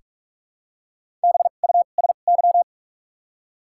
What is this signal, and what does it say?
Morse code, heard as a keyed tone, about 48 wpm, spaced as for 33 wpm, 700 Hz. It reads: BUSY
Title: 2x Character Spacing - Top 500 Words - 45wpm